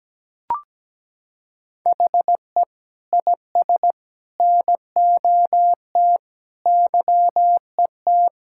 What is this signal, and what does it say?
Morse code, heard as a keyed tone, about 17 wpm, 700 Hz. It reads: HE IS NOT YET
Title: Sentences from Top 1000 Words - Review Part 6 - 17wpm